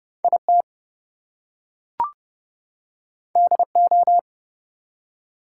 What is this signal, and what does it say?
Morse code, heard as a keyed tone, about 30 wpm, 700 Hz. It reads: IT DO
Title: Two Letter Words - 30wpm